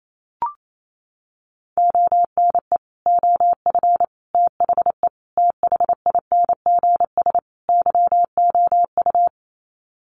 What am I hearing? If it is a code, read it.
Morse code, heard as a keyed tone, about 28 wpm, 700 Hz. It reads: ONE OF THE THINGS YOU
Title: Sets of 5 Words Encore2 28wpm